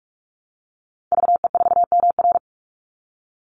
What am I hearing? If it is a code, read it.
Morse code, heard as a keyed tone, about 44 wpm, 700 Hz. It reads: VE4MR
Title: Call Signs - 45wpm